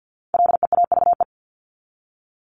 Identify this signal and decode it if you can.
Morse code, heard as a keyed tone, about 50 wpm, 700 Hz. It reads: LEAVE
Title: Top 500 Words 50wpm